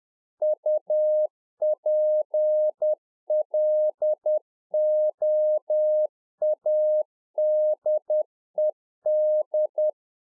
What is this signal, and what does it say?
Morse code, heard as a keyed tone, about 10 wpm, 610 Hz. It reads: UPLOADED